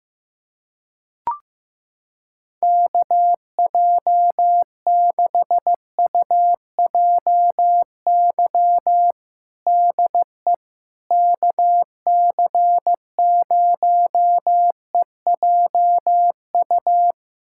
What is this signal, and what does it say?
Morse code, heard as a keyed tone, about 15 wpm, 700 Hz. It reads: KJ6UJY DE KC0EJU
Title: USA Callsign DE Callsign - 6 Characters - 15wpm